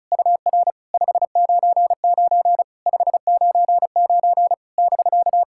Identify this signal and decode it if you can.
Morse code, heard as a keyed tone, about 34 wpm, 690 Hz. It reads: UR 599 599 <BK>
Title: SOTA QSO Practice 35wpm